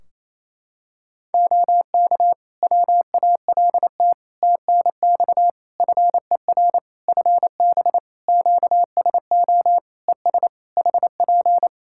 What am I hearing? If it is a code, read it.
Morse code, heard as a keyed tone, about 27 wpm, 700 Hz. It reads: OK WALT TNX FER FB QSO ES HP
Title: Practice QSO - Exchange 4 - Ending - 28wpm